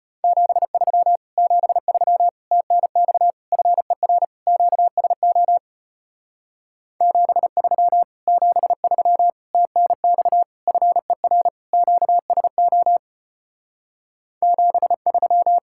Morse code, heard as a keyed tone, about 33 wpm, 700 Hz. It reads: 73 73 TNX FER QSO 73 73 TNX FER QSO 73